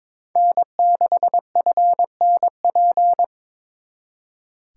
Morse code, heard as a keyed tone, about 25 wpm, 700 Hz. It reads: N6FNP
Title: Speed Racing - USA 1x3 Call Signs 22wpm